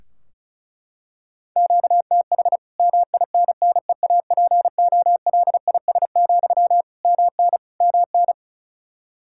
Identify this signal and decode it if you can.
Morse code, heard as a keyed tone, about 34 wpm, 700 Hz. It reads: QTH MINNEAPOLIS, MN MN
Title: Practice QSO - Exchange 1 - Beginning - 35wpm